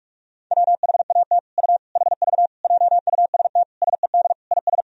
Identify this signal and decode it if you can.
Morse code, heard as a keyed tone, about 42 wpm, 700 Hz. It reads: WHAT U HV JUST SED IS
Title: Sentences from Top 200 Words and Abbreviations 45wpm